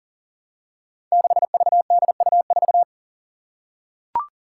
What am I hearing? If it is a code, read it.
Morse code, heard as a keyed tone, about 40 wpm, 700 Hz. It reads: 6VDU4
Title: Five Challenging Letter-Number Combinations 40wpm